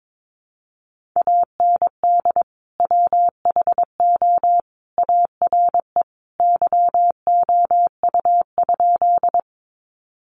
Morse code, heard as a keyed tone, about 22 wpm, 700 Hz. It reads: AND WHO ARE YOU?
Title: Sentences from Top 100 Words 22wpm